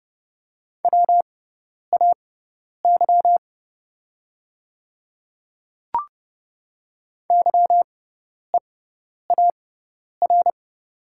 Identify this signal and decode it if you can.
Morse code, heard as a keyed tone, about 31 wpm, spaced as for 10 wpm, 700 Hz. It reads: WAY YEAR
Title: 6x Character Spacing - Top 200 Words - 30wpm